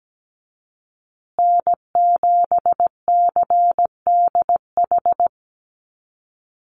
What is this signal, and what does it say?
Morse code, heard as a keyed tone, about 17 wpm, 700 Hz. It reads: N7CDH
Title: USA Call Signs Repeated 3x 17wpm